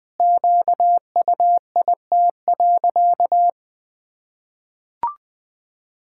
Morse code, heard as a keyed tone, about 19 wpm, 700 Hz. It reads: QUIT.
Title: Pangram Challenge v2 - 20wpm